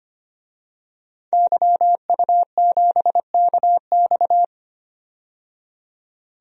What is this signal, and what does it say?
Morse code, heard as a keyed tone, about 25 wpm, 700 Hz. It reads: YU7KX
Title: DXCC-100 Call Signs - 5 Characters - 25wpm